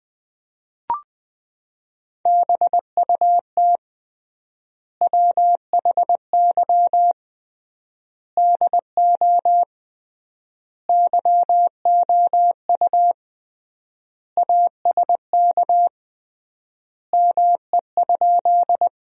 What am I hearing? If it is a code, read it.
Morse code, heard as a keyed tone, about 21 wpm, 700 Hz. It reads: BUT WHY DO YOU ASK ME?
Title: 3x Word-Spacing - Sentences from Top 200 Words - 20wpm